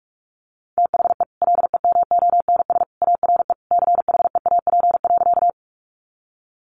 Morse code, heard as a keyed tone, about 44 wpm, 700 Hz. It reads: THE LEMONS ARE CHEAP.
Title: Additional Easy Phrases 45wpm